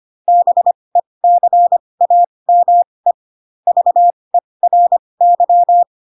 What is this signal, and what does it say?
Morse code, heard as a keyed tone, about 25 wpm, spaced as for 20 wpm, 700 Hz. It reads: BECAME VERY
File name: Pinocchio / Pinocchio ch3 15 25 ik1hge